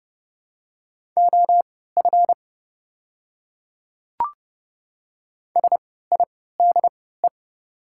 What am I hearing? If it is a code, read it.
Morse code, heard as a keyed tone, about 30 wpm, spaced as for 17 wpm, 700 Hz. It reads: OF SIDE
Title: 3x Character Spacing - Top 100 Words - 30wpm